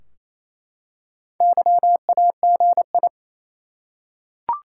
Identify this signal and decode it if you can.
Morse code, heard as a keyed tone, about 28 wpm, 700 Hz. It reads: YAGI
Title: Top 180 QSO Elements 28wpm